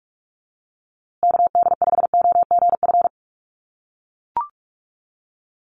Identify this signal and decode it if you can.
Morse code, heard as a keyed tone, about 44 wpm, 700 Hz. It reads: KD5OGF
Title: USA Call Signs Repeated 2x 45wpm